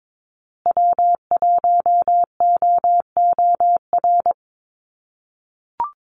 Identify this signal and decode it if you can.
Morse code, heard as a keyed tone, about 22 wpm, 700 Hz. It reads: W1OOR
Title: USA Call Signs Repeated 2x 22wpm